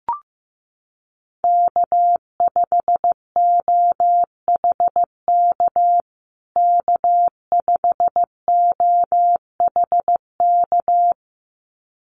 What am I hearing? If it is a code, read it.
Morse code, heard as a keyed tone, about 15 wpm, 700 Hz. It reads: K5OHK K5OHK
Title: USA Call Signs Repeated 2x 15wpm